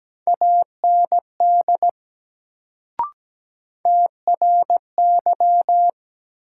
Morse code, heard as a keyed tone, about 17 wpm, 700 Hz. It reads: AND TRY
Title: Sets of 2 Words - New Word Drill - 17wpm